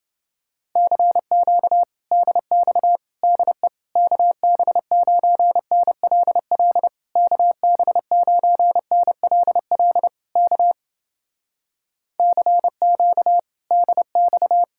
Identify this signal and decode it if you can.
Morse code, heard as a keyed tone, about 29 wpm, 700 Hz. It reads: CQ DX DE KB9NLL KB9NLL K CQ DX